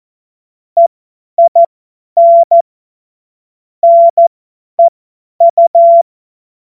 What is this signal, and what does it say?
Morse code, heard as a keyed tone, about 14 wpm, spaced as for 10 wpm, 680 Hz. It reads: EIN NEU
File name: Morsekurs / Witze / Witz 24.Oktober 2023 50 F70